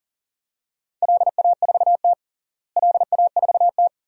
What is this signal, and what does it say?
Morse code, heard as a keyed tone, about 40 wpm, 700 Hz. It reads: LA4T LA4T
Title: DXCC-100 Call Signs - 4 Characters - Repeated 2x - 40wpm